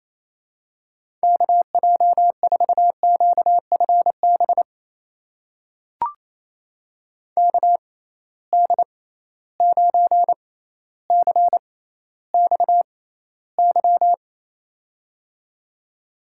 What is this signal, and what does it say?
Morse code, heard as a keyed tone, about 28 wpm, spaced as for 14 wpm, 700 Hz. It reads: KJ4QFB KD9CXY
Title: USA Call Signs - 6 Characters - 6x Character Spacing - 28wpm